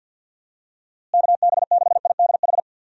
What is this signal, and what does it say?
Morse code, heard as a keyed tone, about 50 wpm, 700 Hz. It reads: KB6IDH
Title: USA Call Signs - 6 Characters - Repeated 2x - 50wpm